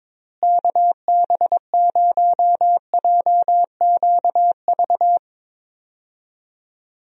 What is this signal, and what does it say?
Morse code, heard as a keyed tone, about 22 wpm, 700 Hz. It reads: KB0JQV
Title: USA Call Signs - 6 Characters - Repeated 3x - 22wpm